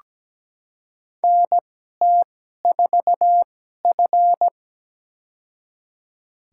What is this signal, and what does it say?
Morse code, heard as a keyed tone, about 17 wpm, spaced as for 12 wpm, 700 Hz. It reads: NT4F
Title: USA Call Signs - 4 Characters - 2x Character Spacing - 17wpm